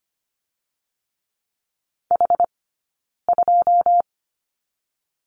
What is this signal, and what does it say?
Morse code, heard as a keyed tone, about 25 wpm, 700 Hz. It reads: H 2
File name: Lesson 15d - Number 5 - Two Characters 25wpm